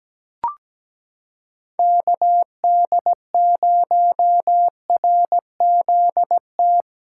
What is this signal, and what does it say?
Morse code, heard as a keyed tone, about 17 wpm, 700 Hz. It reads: KD0RZT
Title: USA Callsign DE Callsign 17wpm